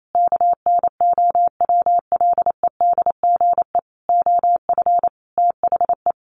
Morse code, heard as a keyed tone, about 28 wpm, 700 Hz. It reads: KNOWLEDGE OF THE
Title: Sentences from Top 1500 Words 28wpm